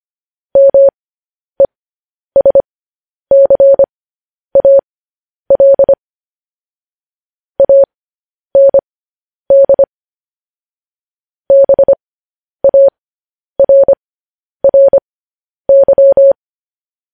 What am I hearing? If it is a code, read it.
Morse code, heard as a keyed tone, about 25 wpm, spaced as for 10 wpm, 550 Hz. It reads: MESCAL AND BARRY
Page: News Headlines in Morse Code at 10 WPM